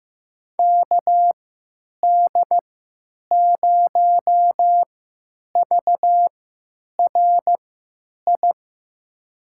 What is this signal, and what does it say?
Morse code, heard as a keyed tone, about 15 wpm, spaced as for 9 wpm, 700 Hz. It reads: KD0VRI
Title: USA Call Signs - 3x Character Spacing - 15wpm